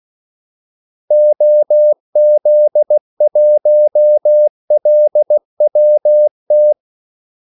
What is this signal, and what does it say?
Morse code, heard as a keyed tone, about 16 wpm, 600 Hz. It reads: OZ1LWT
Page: Kaldesignaler 16 wpm